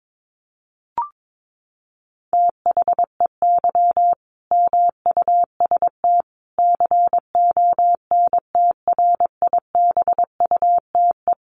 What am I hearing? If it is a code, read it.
Morse code, heard as a keyed tone, about 22 wpm, 700 Hz. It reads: THEY MUST CONTRIBUTE
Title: Sentences from Top 1400 Words 22wpm